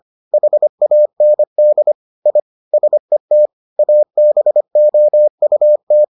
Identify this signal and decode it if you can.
Morse code, heard as a keyed tone, about 25 wpm, 600 Hz. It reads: HAND I SET ABOUT